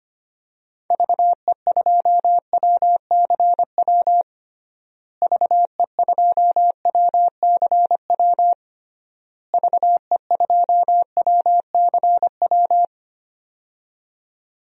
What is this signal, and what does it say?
Morse code, heard as a keyed tone, about 25 wpm, 700 Hz. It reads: VE2WCW VE2WCW VE2WCW
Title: DXCC-100 Call Signs - 6 Characters - Repeated 3x - 25wpm